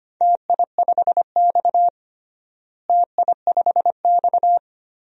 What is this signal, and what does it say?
Morse code, heard as a keyed tone, about 25 wpm, 700 Hz. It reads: TI5X TI5X
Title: DXCC-100 Call Signs - 4 Characters - Repeated 2x - 25wpm